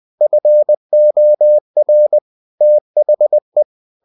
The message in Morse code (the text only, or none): FOR THE